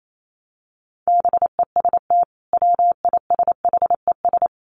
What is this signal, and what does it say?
Morse code, heard as a keyed tone, about 28 wpm, 700 Hz. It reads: BEST WISHES
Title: Common QSO Phrases 28wpm